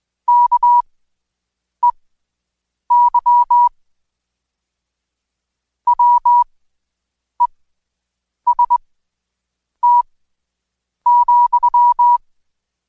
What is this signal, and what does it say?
Morse code, heard as a keyed tone, about 21 wpm, spaced as for 7 wpm, 970 Hz. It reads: KEY WEST,
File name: Code Quick / AudioDownld2 / Seg113-Day18-DayDrill 18